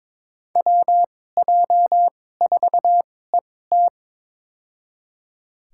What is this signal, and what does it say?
Morse code, heard as a keyed tone, about 22 wpm, spaced as for 16 wpm, 700 Hz. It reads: WJ4ET
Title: USA Call Signs - 2x Character Spacing - 22wpm